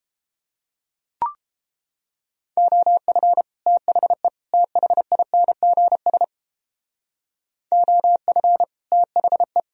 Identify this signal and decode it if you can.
Morse code, heard as a keyed tone, about 29 wpm, 700 Hz. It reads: OF THE THINGS OF THE